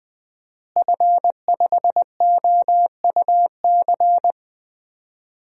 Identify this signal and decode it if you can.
Morse code, heard as a keyed tone, about 20 wpm, 700 Hz. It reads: F5OUC
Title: DXCC-100 Call Signs - 5 Characters - Repeated 2x - 20wpm